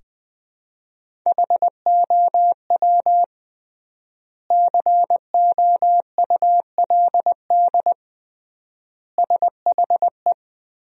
Morse code, heard as a keyed tone, about 20 wpm, 700 Hz. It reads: HOW COULD SHE
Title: 3x Word-Spacing - Sentences from Top 200 Words - 20wpm